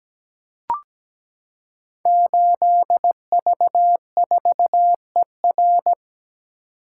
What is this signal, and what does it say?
Morse code, heard as a keyed tone, about 17 wpm, 700 Hz. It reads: 8V4ER
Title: Code Groups - 5 Letter:Numbers - 17wpm